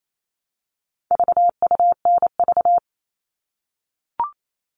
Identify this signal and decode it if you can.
Morse code, heard as a keyed tone, about 28 wpm, 700 Hz. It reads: VUNV
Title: All Cut Numbers - 4-Digit 28wpm